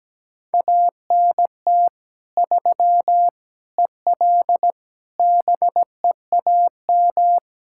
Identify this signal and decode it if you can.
Morse code, heard as a keyed tone, about 17 wpm, 700 Hz. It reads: ANT 3 EL BEAM
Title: Practice QSO - Exchange 2 - Info - 17wpm